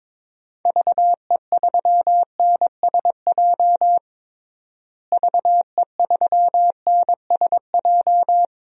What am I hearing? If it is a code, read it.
Morse code, heard as a keyed tone, about 22 wpm, 700 Hz. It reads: VE3NSJ VE3NSJ
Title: DXCC-100 Call Signs - 6 Characters - Repeated 3x - 22wpm